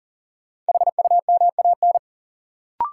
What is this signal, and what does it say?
Morse code, heard as a keyed tone, about 40 wpm, 700 Hz. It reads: HUMAN
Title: Sets of 5 Words - New Word Drill - 40wpm